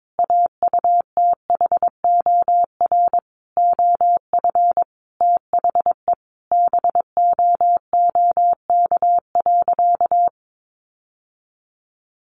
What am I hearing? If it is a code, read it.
Morse code, heard as a keyed tone, about 22 wpm, 700 Hz. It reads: AUTHOR OF THE BOOK.
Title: Sentences from Top 1200 Words 22wpm